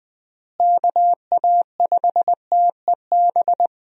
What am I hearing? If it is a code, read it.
Morse code, heard as a keyed tone, about 20 wpm, 700 Hz. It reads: KA5TEB
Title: USA Call Signs - 6 Characters - Repeated 2x - 20wpm